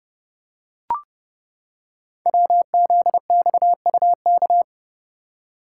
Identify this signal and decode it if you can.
Morse code, heard as a keyed tone, about 30 wpm, 700 Hz. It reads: WZXUK
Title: Code Groups - 5 Letter:Numbers - 30wpm